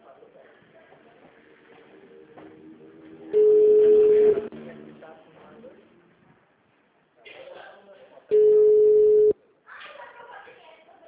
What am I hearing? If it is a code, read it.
Ringback tone (Europe, 425 Hz), with speech.
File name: out- -2016- - - .36730